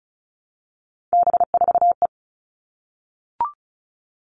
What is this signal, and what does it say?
Morse code, heard as a keyed tone, about 34 wpm, 700 Hz. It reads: B4E
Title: All Cut Numbers - 3-Digit 35wpm